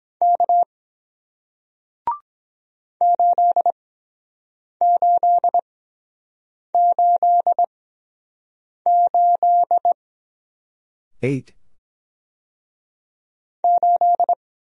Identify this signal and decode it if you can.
Morse code, heard as a keyed tone, about 23 wpm, spaced as for 7 wpm, 700 Hz. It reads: K 8888 8